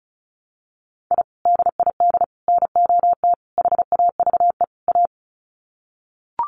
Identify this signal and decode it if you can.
Morse code, heard as a keyed tone, about 34 wpm, 700 Hz. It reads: I DID NOT HAVE A
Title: Sets of 5 Words 35wpm